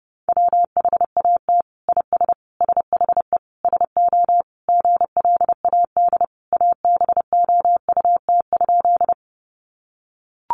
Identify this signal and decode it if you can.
Morse code, heard as a keyed tone, about 30 wpm, 700 Hz. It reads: WHAT IS SHE SO GLAD ABOUT?
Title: Sentences from Top 800 Words 30wpm